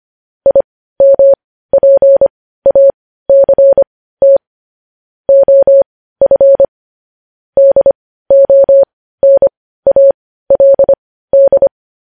Morse code, heard as a keyed tone, about 25 wpm, spaced as for 15 wpm, 550 Hz. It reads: IMPACT OF DONALD
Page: news-headlines-in-morse-code-at-15-wpm